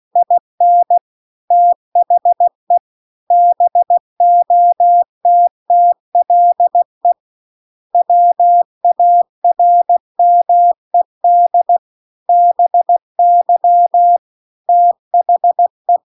Morse code, heard as a keyed tone, about 16 wpm, 700 Hz. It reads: IN THE BOTTLE WARMED BY THE
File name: CW-The-Big-Twoheaded-River-2-16WPM